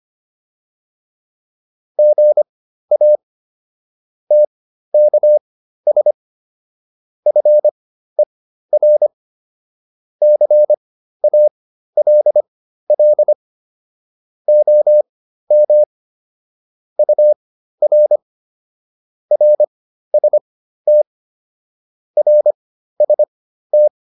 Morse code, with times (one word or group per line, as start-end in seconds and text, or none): 2.0-3.2 GA
4.3-6.1 TKS
7.3-9.1 FER
10.2-13.3 CALL
14.5-15.8 OM
17.0-18.2 UR
19.3-21.0 RST
22.2-23.9 RST